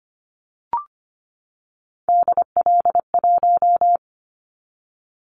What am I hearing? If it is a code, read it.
Morse code, heard as a keyed tone, about 25 wpm, 700 Hz. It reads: DL1
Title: Callsign Prefixes for Top 50 DXCC - 25wpm